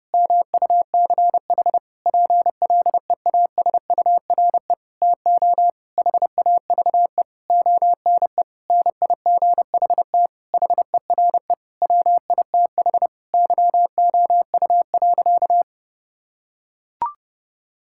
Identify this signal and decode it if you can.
Morse code, heard as a keyed tone, about 30 wpm, 700 Hz. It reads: MUCH PLEASURE TO HAVE ONE NIGHT HERE WITH YOU.
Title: Sentences from Top 1600 Words 30wpm